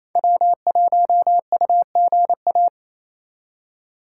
Morse code, heard as a keyed tone, about 35 wpm, 700 Hz. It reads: W1UGA
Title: Speed Racing - USA Call Signs 28wpm